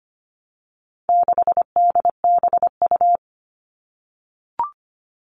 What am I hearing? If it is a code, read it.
Morse code, heard as a keyed tone, about 25 wpm, 700 Hz. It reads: 6DBU
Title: All Cut Numbers - 4-Digit 25wpm